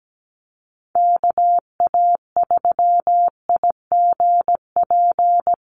Morse code, heard as a keyed tone, about 17 wpm, 700 Hz. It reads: KA3IGP
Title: USA Call Signs Repeated 2x 17wpm